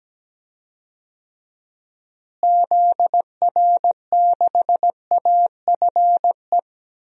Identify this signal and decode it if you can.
Morse code, heard as a keyed tone, about 17 wpm, 700 Hz. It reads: ZR6AFE
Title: DXCC-100 Call Signs - 6 Characters - Repeated 2x - 17wpm